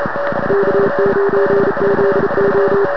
Morse code, heard as a keyed tone, about 40 wpm, 410 Hz, with noise. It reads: K9ZO